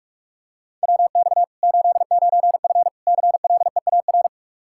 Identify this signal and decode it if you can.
Morse code, heard as a keyed tone, about 44 wpm, 700 Hz. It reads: WX 89F CLEAR
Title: Practice QSO - Exchange 2 - Info - 45wpm